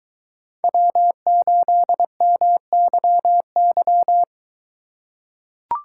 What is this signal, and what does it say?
Morse code, heard as a keyed tone, about 22 wpm, 700 Hz. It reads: W8MYY